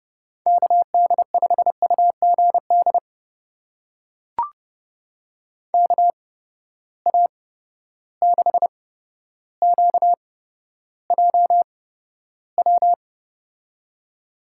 Morse code, heard as a keyed tone, about 31 wpm, spaced as for 13 wpm, 700 Hz. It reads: KD5UGD KA6QJW
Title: USA Call Signs - 6 Characters - 8x Character Spacing - 30wpm